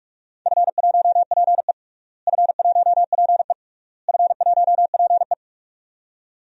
Morse code, heard as a keyed tone, about 46 wpm, 700 Hz. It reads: F1PE F1PE F1PE
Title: DXCC-100 Call Signs - 4 Characters - Repeated 3x - 45wpm